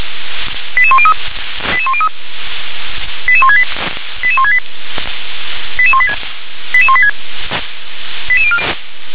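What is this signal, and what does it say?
Five-tone selective call (ZVEI), heard as quick runs of five tones, tones of about 70 ms. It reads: 88103 88103 88168 88168 88186 88186
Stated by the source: ZVEI five-tone selective call 88006, received by SDR.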